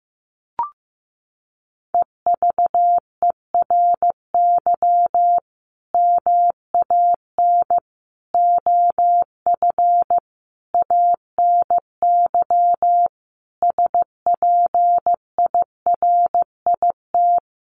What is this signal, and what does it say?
Morse code, heard as a keyed tone, about 15 wpm, 700 Hz. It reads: EVERY MAN OF ANY SPIRIT
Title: Sentences from Top 1400 Words 15wpm